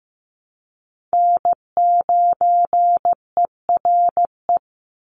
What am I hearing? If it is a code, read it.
Morse code, heard as a keyed tone, about 15 wpm, 700 Hz. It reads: N9ERE
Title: USA Call Signs 15wpm